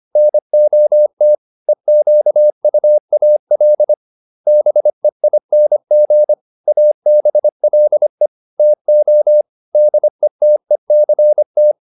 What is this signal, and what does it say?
Morse code, heard as a keyed tone, about 25 wpm, 600 Hz. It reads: NOT EQUAL BEING ABLE TO DETECT